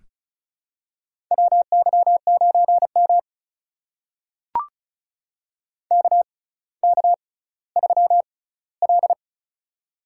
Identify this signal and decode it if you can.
Morse code, heard as a keyed tone, about 36 wpm, spaced as for 18 wpm, 700 Hz. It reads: WY9M KK3L
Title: USA Call Signs - 4 Characters - 6x Character Spacing - 35wpm